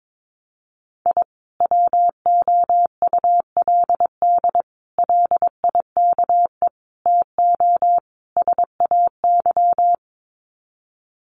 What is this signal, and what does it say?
Morse code, heard as a keyed tone, about 22 wpm, 700 Hz. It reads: I WOULD LIKE TO SAY
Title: Sets of 5 Words Encore2 22wpm